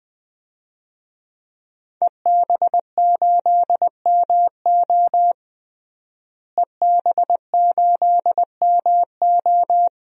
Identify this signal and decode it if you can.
Morse code, heard as a keyed tone, about 20 wpm, 700 Hz. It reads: EB8MO EB8MO
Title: DXCC-100 Call Signs - 5 Characters - Repeated 2x - 20wpm